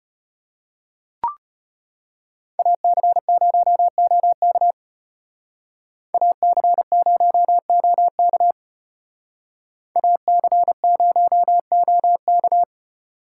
Morse code, heard as a keyed tone, about 32 wpm, 700 Hz. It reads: AC0OK AC0OK AC0OK